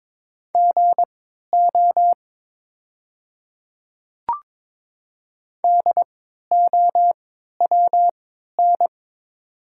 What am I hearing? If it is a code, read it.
Morse code, heard as a keyed tone, about 22 wpm, spaced as for 12 wpm, 700 Hz. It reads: GO DOWN
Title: 3x Character Spacing - Top 100 Words - 22wpm